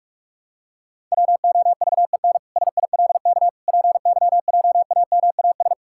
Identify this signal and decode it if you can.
Morse code, heard as a keyed tone, about 41 wpm, 700 Hz. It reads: WOVEN SILK PYJAMAS
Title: Pangram Challenge v2 - 45wpm